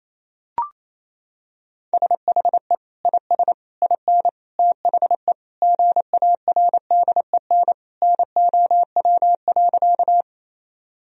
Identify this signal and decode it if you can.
Morse code, heard as a keyed tone, about 28 wpm, 700 Hz. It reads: SHE IS IN THE GARDEN NOW.
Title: Sentences from Top 1000 Words - Review Part 4 - 28wpm